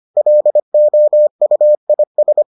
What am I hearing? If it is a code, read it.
Morse code, heard as a keyed tone, about 25 wpm, 600 Hz. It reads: LOUIS